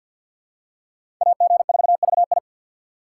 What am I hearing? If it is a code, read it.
Morse code, heard as a keyed tone, about 52 wpm, 690 Hz. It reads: AG4VI
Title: USA Call Signs - 5 Characters - 3x Character Spacing - 50wpm